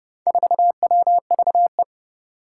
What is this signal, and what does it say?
Morse code, heard as a keyed tone, about 30 wpm, 700 Hz. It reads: 4WVE
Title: Four Challenging Letter-Number Combinations 30wpm